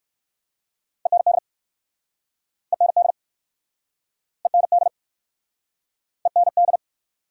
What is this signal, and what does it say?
Morse code, heard as a keyed tone, about 59 wpm, 700 Hz. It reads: END END END END